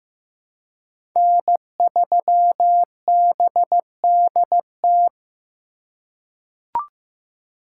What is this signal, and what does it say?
Morse code, heard as a keyed tone, about 15 wpm, 700 Hz. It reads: N3BDT
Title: USA Call Signs - 5 Characters - 2x Character Spacing - 15wpm